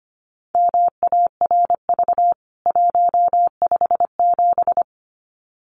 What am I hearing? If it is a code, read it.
Morse code, heard as a keyed tone, about 25 wpm, 700 Hz. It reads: MARV 157
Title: CWT Exchange 25wpm